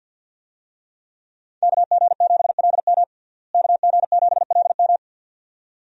Morse code, heard as a keyed tone, about 51 wpm, 700 Hz. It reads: KG7LM KG7LM
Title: USA Call Signs - 5 Characters - Repeated 2x - 50wpm